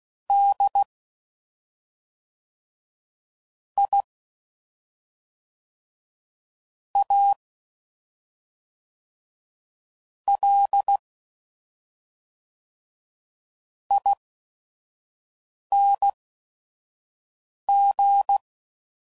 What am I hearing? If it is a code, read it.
Morse code, heard as a keyed tone, about 16 wpm, spaced as for 5 wpm, 800 Hz. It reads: D I A L ING
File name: Suffixes / Special ING's / SlowToFaster ING's